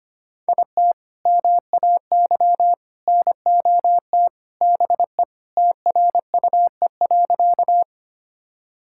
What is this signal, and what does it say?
Morse code, heard as a keyed tone, about 25 wpm, 700 Hz. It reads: IT MAY NOT BE TRUE.
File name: practice / 25wpm / Sentences with 3-Word Phrases - 25wpm